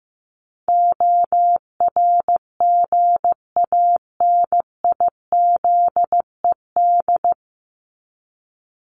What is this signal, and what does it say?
Morse code, heard as a keyed tone, about 15 wpm, 700 Hz. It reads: ORGANIZED
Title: Top 1001-1250 Words 15wpm